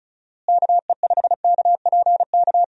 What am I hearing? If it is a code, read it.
Morse code, heard as a keyed tone, about 36 wpm, 700 Hz. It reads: KE5KPK
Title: USA Call Signs - 6 Characters - Repeated 2x - 35wpm